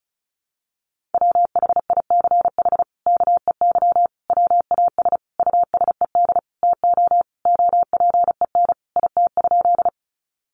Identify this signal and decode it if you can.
Morse code, heard as a keyed tone, about 34 wpm, 700 Hz. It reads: WHICH KEY WAS USED TO OPEN IT?
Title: Sentences from Top 600 Words 35wpm